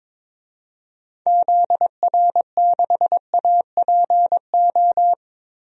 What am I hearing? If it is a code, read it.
Morse code, heard as a keyed tone, about 22 wpm, 700 Hz. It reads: ZR6APO
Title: DXCC-100 Call Signs - 6 Characters - Repeated 3x - 22wpm